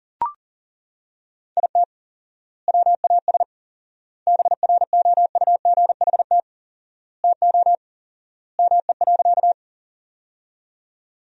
Morse code, heard as a keyed tone, about 40 wpm, 700 Hz. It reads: IT WAS BROUGHT TO ME.
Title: 4x Word-Spacing - Sentences from Top 500 Words - 40wpm